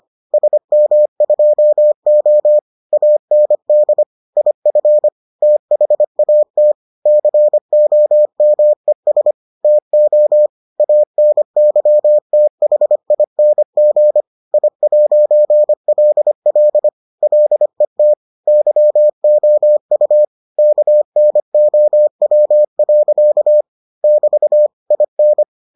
Morse code, heard as a keyed tone, about 25 wpm, 600 Hz. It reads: SM2O AND IF THAT COMES TO ANYTHING I'LL LET YOU KNOW. <BT> IN